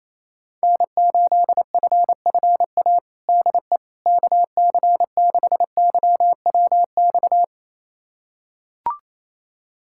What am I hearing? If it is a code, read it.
Morse code, heard as a keyed tone, about 28 wpm, 700 Hz. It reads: N8FFA DE KC6YWX
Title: USA Callsign DE Callsign 28wpm